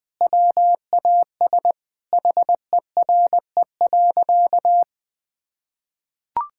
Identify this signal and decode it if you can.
Morse code, heard as a keyed tone, about 20 wpm, 700 Hz. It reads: WAS HERE.
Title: Sentences with 3-Word Phrases - 20wpm